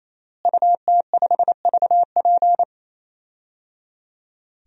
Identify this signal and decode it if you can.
Morse code, heard as a keyed tone, about 28 wpm, 700 Hz. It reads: UT5VP
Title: DXCC-100 Call Signs - 5 Characters - Repeated 3x - 28wpm